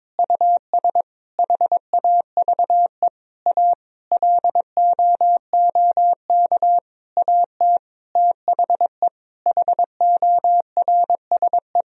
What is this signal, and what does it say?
Morse code, heard as a keyed tone, about 22 wpm, 700 Hz. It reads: US HAVE A LOOK AT THE HORSE
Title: Sentences from Top 1000 Words - Review Part 1 - 22wpm